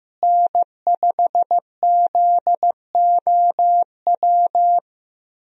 Morse code, heard as a keyed tone, about 15 wpm, 700 Hz. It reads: N5ZOW
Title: USA Call Signs - 5 Characters - Repeated 2x - 15wpm